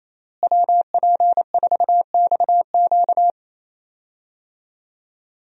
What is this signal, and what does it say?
Morse code, heard as a keyed tone, about 27 wpm, 700 Hz. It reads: WP4XQ
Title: USA Call Signs - 5 Characters - Repeated 2x - 28wpm